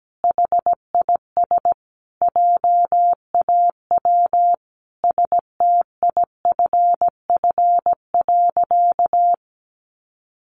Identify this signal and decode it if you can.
Morse code, heard as a keyed tone, about 17 wpm, 700 Hz. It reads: HIS JAW STIFF.
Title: A Sound Of Thunder 17wpm